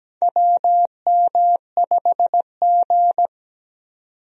Morse code, heard as a keyed tone, about 17 wpm, 700 Hz. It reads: WM5G
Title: USA Call Signs - 4 Characters - 17wpm